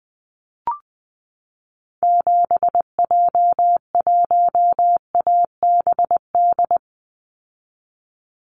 Morse code, heard as a keyed tone, about 20 wpm, 700 Hz. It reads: 7J1ABD
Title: Call Signs Repeated 3x - 20wpm